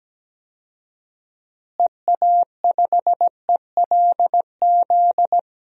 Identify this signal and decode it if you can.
Morse code, heard as a keyed tone, about 17 wpm, 700 Hz. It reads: EA5ELZ
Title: DXCC-100 Call Signs - 6 Characters - Repeated 3x - 17wpm